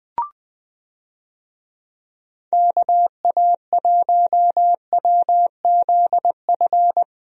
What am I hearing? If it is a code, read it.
Morse code, heard as a keyed tone, about 20 wpm, 700 Hz. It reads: KA1WZF
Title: USA Call Signs - 6 Characters - Repeated 3x - 20wpm